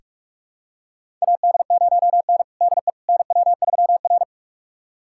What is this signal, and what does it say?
Morse code, heard as a keyed tone, about 42 wpm, 700 Hz. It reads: AD0N DE NW3R
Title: USA Callsign DE Callsign - 4 Characters - 45wpm